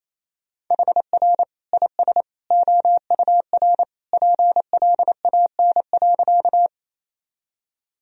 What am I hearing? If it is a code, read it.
Morse code, heard as a keyed tone, about 28 wpm, 700 Hz. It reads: HR IS OUR PLAN.
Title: Sentences from Top 500 Words and Abbreviations 28wpm